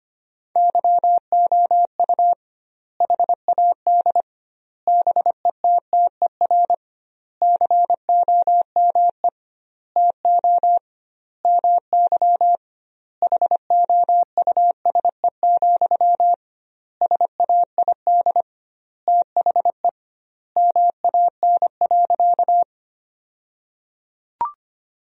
Morse code, heard as a keyed tone, about 25 wpm, 700 Hz. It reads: YOU HAD BETTER COME TO MY HOUSE, SAID THE MAN.
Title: 2x Word-Spacing - Sentences from Top 400 Words - 25wpm